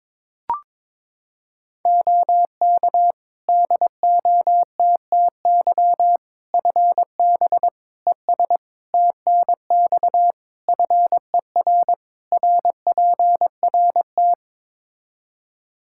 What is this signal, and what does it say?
Morse code, heard as a keyed tone, about 21 wpm, 700 Hz. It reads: OK DOTTY FB ES TNX FER RPRT
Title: K1USN SST QSO Practice 22wpm